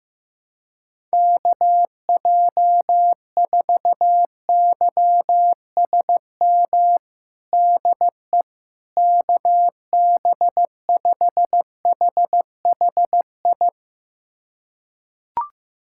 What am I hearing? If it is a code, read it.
Morse code, heard as a keyed tone, about 16 wpm, 700 Hz. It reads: KJ4YSM DE KB5HHI
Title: USA Callsign DE Callsign - 6 Characters - 15wpm